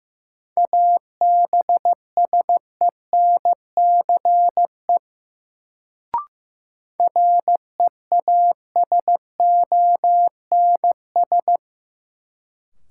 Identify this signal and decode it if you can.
Morse code, heard as a keyed tone, about 15 wpm, 700 Hz. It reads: ABSENCE REASONS
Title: Sets of 4 Words - New Word Drill - 15wpm